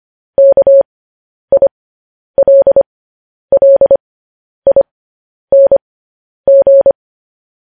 Morse code, heard as a keyed tone, about 25 wpm, spaced as for 10 wpm, 550 Hz. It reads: KILLING
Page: News Headlines in Morse Code at 10 WPM